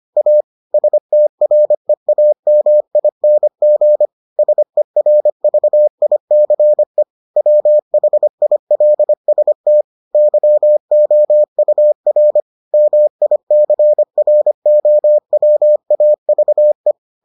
Morse code, heard as a keyed tone, about 25 wpm, 600 Hz. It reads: A STREAMING SERVICE WHILST YOUR MICROWAVE